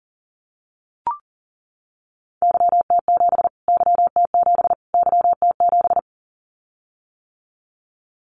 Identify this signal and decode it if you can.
Morse code, heard as a keyed tone, about 40 wpm, 700 Hz. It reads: YT7 YT7 YT7
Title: Callsign Prefixes for Top 25 DXCC - Repeated 3x - 40wpm